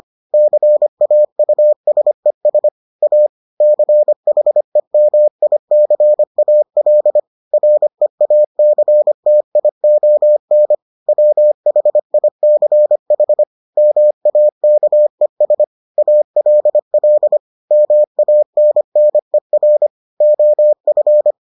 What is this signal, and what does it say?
Morse code, heard as a keyed tone, about 25 wpm, 600 Hz. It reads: CAUSES A CHEMICAL REACTION WHICH MAKES ALL MANNER OF